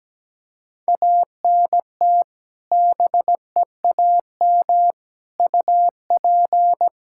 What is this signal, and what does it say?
Morse code, heard as a keyed tone, about 17 wpm, 700 Hz. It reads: ANT BEAM UP
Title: Practice QSO - Exchange 2 - Info - 17wpm